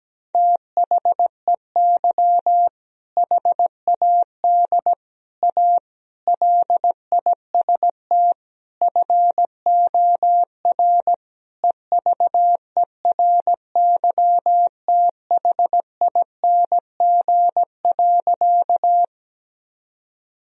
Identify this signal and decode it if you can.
Morse code, heard as a keyed tone, about 17 wpm, 700 Hz. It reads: THEY HAD A LIST FOR EVERYTHING.
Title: Sentences from Top 1000 Words - Review Part 2 - 17wpm